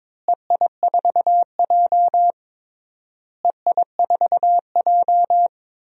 Morse code, heard as a keyed tone, about 21 wpm, 700 Hz. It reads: EI4J EI4J
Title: DXCC-100 Call Signs - 4 Characters - Repeated 3x - 22wpm